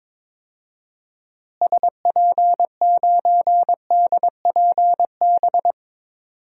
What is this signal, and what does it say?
Morse code, heard as a keyed tone, about 21 wpm, 700 Hz. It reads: SP9DPB
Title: DXCC-100 Call Signs - 6 Characters - Repeated 3x - 22wpm